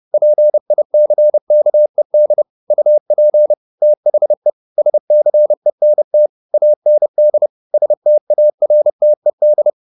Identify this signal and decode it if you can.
Morse code, heard as a keyed tone, about 30 wpm, 600 Hz. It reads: PICKED UP THE SCENT AND STARTED